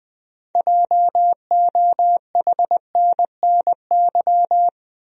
Morse code, heard as a keyed tone, about 20 wpm, 700 Hz. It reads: JOHNNY
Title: USA Common Names - 20wpm